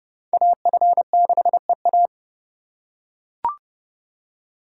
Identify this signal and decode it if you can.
Morse code, heard as a keyed tone, about 31 wpm, 690 Hz. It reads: AF6EA
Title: USA Call Signs - 5 Characters - 2x Character Spacing - 30wpm